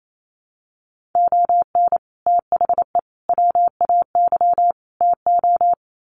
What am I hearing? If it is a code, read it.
Morse code, heard as a keyed tone, about 28 wpm, 700 Hz. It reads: ON THE WAY TO
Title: Sets of 4 Words 28wpm